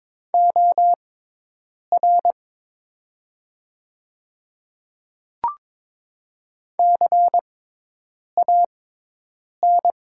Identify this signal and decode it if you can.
Morse code, heard as a keyed tone, about 22 wpm, spaced as for 8 wpm, 700 Hz. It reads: OR CAN
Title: 6x Character Spacing - Top 100 Words - 22wpm